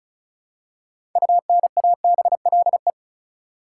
Morse code, heard as a keyed tone, about 36 wpm, 700 Hz. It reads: UNABLE
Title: Six Letter Words - 35wpm